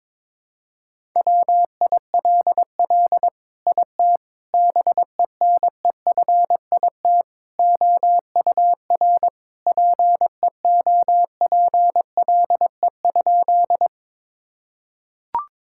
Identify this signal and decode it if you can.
Morse code, heard as a keyed tone, about 22 wpm, 700 Hz. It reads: WILL IT BENEFIT OUR PEOPLE?
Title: Sentences from Top 1000 Words - Review Part 6 - 22wpm